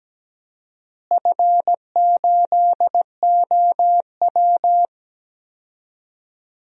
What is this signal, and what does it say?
Morse code, heard as a keyed tone, about 17 wpm, 700 Hz. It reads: F8OW
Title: DXCC-100 Call Signs - 4 Characters - Repeated 3x - 17wpm